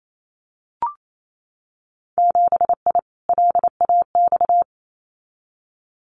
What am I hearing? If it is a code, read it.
Morse code, heard as a keyed tone, about 28 wpm, 690 Hz. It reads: 7I LAX
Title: Winter Field Day Exchanges Encore 28wpm